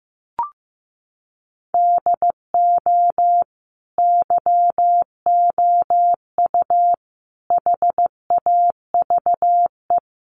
Morse code, heard as a keyed tone, about 15 wpm, 700 Hz. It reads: DO YOU HAVE
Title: Sentences from Top 1500 Words 15wpm